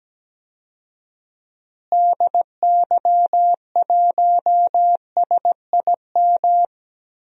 Morse code, heard as a keyed tone, about 17 wpm, 700 Hz. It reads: DY1SIM
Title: DXCC-100 Call Signs - 6 Characters - Repeated 3x - 17wpm